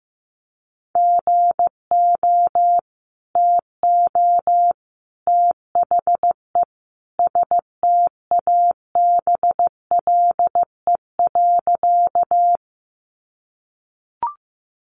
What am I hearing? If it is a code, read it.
Morse code, heard as a keyed tone, about 15 wpm, 700 Hz. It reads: GO TO THE STABLE.
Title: Sentences from Top 1500 Words 15wpm